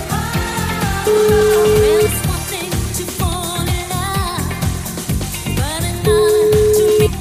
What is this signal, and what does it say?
Ringback tone (Europe, 425 Hz), with noise.